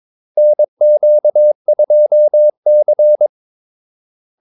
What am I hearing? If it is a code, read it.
Morse code, heard as a keyed tone, about 22 wpm, 600 Hz. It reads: NQ2C
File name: Callsigns-22wpm